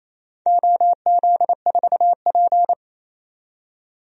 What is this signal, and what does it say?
Morse code, heard as a keyed tone, about 28 wpm, 700 Hz. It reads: OZ4P
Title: DXCC-100 Call Signs - 4 Characters - 28wpm